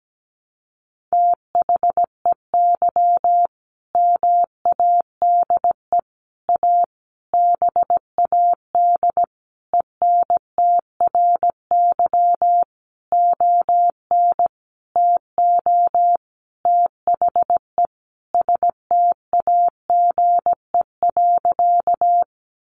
Morse code, heard as a keyed tone, about 17 wpm, 700 Hz. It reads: THEY MADE A BAD ENTRY ON TO THE STAGE.
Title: Sentences from Top 1500 Words 17wpm